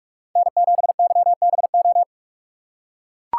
Morse code, heard as a keyed tone, about 44 wpm, 700 Hz. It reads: N7YBO
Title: USA Call Signs - 5 Characters - 4x Character Spacing - 45wpm